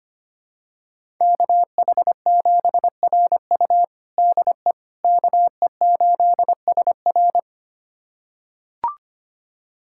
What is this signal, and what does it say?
Morse code, heard as a keyed tone, about 25 wpm, 700 Hz. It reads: KH7RU DE KE8SR
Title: USA Callsign DE Callsign - 5 Characters - 25wpm